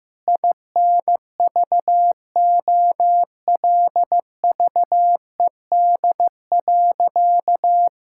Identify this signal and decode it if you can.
Morse code, heard as a keyed tone, about 15 wpm, 700 Hz. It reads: INVOLVED.
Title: Sentences from Top 1000 Words - Review Part 5 - 15wpm